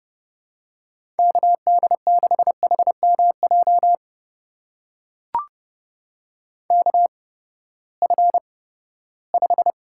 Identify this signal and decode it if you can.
Morse code, heard as a keyed tone, about 30 wpm, spaced as for 17 wpm, 700 Hz. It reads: KD6HMJ KF5
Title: USA Call Signs - 6 Characters - 8x Character Spacing - 30wpm